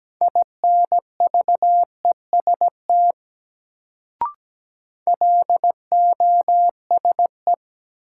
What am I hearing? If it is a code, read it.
Morse code, heard as a keyed tone, about 17 wpm, 700 Hz. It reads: INVEST LOSE
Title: Sentences from Top 1100 Words - New Word Drill Alt - 17wpm